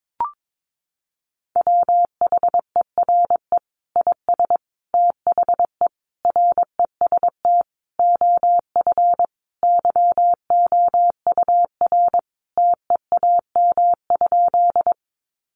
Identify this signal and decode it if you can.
Morse code, heard as a keyed tone, about 22 wpm, 700 Hz. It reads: WHERE IS THE REST OF YOUR TEAM?
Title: Sentences from Top 1500 Words 22wpm